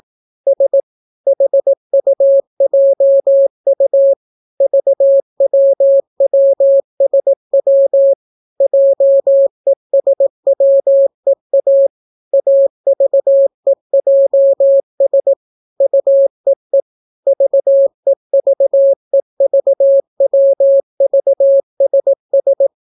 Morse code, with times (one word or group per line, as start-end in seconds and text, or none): 0.5-0.8 S
1.3-4.1 HUJU
4.6-8.1 VWWSW
8.6-11.9 JESWEA
12.3-15.3 AVEJS
15.8-16.8 UEE
17.3-22.7 VEVEVWVSS